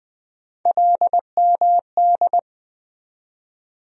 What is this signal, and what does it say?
Morse code, heard as a keyed tone, about 20 wpm, 700 Hz. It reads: LMD
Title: USA Call Sign Postfixes - 3 Letters 20wpm